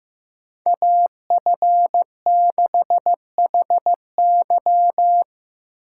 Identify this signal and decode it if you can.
Morse code, heard as a keyed tone, about 15 wpm, 700 Hz. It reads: AF6HY
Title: USA Call Signs - 4x Character Spacing - 15wpm